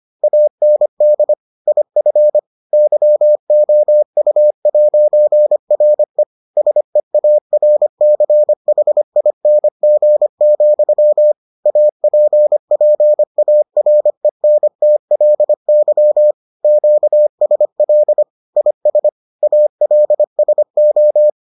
Morse code, heard as a keyed tone, about 25 wpm, 600 Hz. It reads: AND IF YOU'RE SEARCHING, APPARENTLY QSL IS ALSO